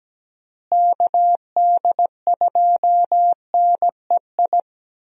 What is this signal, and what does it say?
Morse code, heard as a keyed tone, about 17 wpm, 700 Hz. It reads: KD2NEI
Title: USA Call Signs - 6 Characters - Repeated 2x - 17wpm